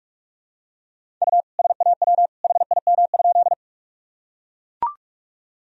Morse code, heard as a keyed tone, about 41 wpm, 700 Hz. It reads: U SAW HIM?
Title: Sentences from Top 300 Words and Abbreviations 45wpm